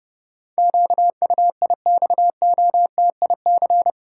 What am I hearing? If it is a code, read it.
Morse code, heard as a keyed tone, about 30 wpm, 700 Hz. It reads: QUIXOTIC
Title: Pangram Challenge v2 - 30wpm